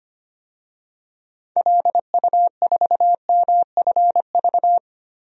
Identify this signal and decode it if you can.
Morse code, heard as a keyed tone, about 25 wpm, 700 Hz. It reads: LU4MFV
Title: DXCC-100 Call Signs - 6 Characters - 25wpm